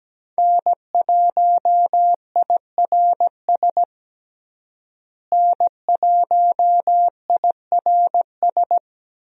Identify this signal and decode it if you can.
Morse code, heard as a keyed tone, about 17 wpm, 700 Hz. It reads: N1IRS N1IRS
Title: USA Call Signs - 5 Characters - Repeated 3x - 17wpm